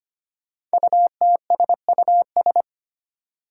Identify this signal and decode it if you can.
Morse code, heard as a keyed tone, about 25 wpm, 700 Hz. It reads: UTSUS
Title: Five Challenging Letter-Number Combinations 25wpm